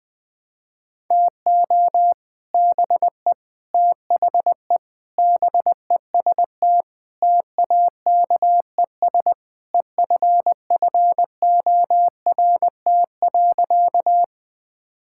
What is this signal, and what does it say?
Morse code, heard as a keyed tone, about 20 wpm, 700 Hz. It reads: TO BE THE BEST TAKES EFFORT.
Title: Sentences with 3-Word Phrases - Spoken Beginning - 20wpm